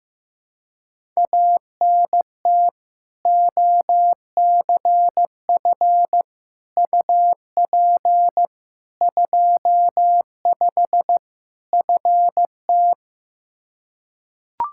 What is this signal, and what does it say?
Morse code, heard as a keyed tone, about 15 wpm, 700 Hz. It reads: ANT OCF UP 25 FT
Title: Practice QSO - Exchange 2 - Info - 15wpm